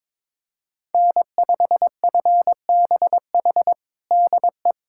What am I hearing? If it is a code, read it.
Morse code, heard as a keyed tone, about 22 wpm, 700 Hz. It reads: N5FBH DE
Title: USA Callsign DE Callsign - 5 Characters - 22wpm